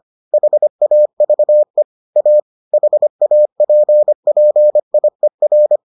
Morse code, heard as a keyed tone, about 25 wpm, 600 Hz. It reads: HAVE A HAPPIER